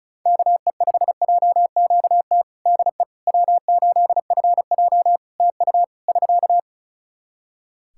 Morse code, heard as a keyed tone, about 34 wpm, 700 Hz. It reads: KE5JQT DE W8FJ TU <SK>
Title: Practice QSO - Exchange 4 - Ending - 35wpm